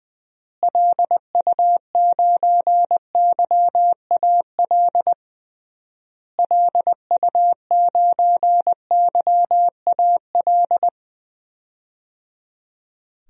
Morse code, heard as a keyed tone, about 19 wpm, 700 Hz. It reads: LU9YAL LU9YAL
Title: DXCC-100 Call Signs - 6 Characters - Repeated 3x - 20wpm